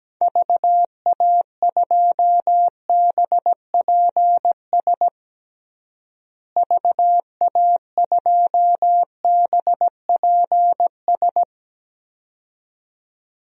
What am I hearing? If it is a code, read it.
Morse code, heard as a keyed tone, about 17 wpm, 700 Hz. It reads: VA2BPS VA2BPS
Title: DXCC-100 Call Signs - 6 Characters - Repeated 2x - 17wpm